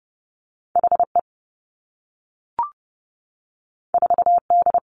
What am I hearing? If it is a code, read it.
Morse code, heard as a keyed tone, about 30 wpm, 700 Hz. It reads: H E 4 D
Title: Two Challenging Letter-Number Combinations 30wpm